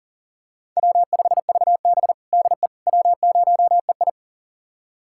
Morse code, heard as a keyed tone, about 40 wpm, 700 Hz. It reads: W5VB DE W0EI
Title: USA Callsign DE Callsign - 4 Characters - 40wpm